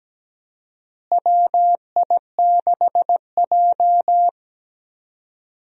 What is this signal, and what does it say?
Morse code, heard as a keyed tone, about 17 wpm, 700 Hz. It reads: WI6J
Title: USA Call Signs - 4 Characters - Repeated 2x - 17wpm